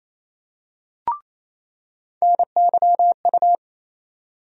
Morse code, heard as a keyed tone, about 28 wpm, 700 Hz. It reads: NYU
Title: USA Call Sign Postfixes - 3 Letters 28wpm